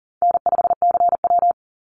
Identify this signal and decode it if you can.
Morse code, heard as a keyed tone, about 40 wpm, 700 Hz. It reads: N5CW
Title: Easy Call Signs - 40wpm